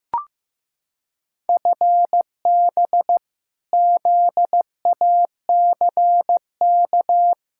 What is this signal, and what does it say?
Morse code, heard as a keyed tone, about 15 wpm, 700 Hz. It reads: FB ZACK
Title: Practice QSO - Exchange 3 - More Info - 15wpm